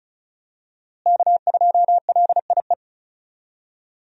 Morse code, heard as a keyed tone, about 34 wpm, 700 Hz. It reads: K2LIE
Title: USA Call Signs - 5 Characters - Repeated 2x - 35wpm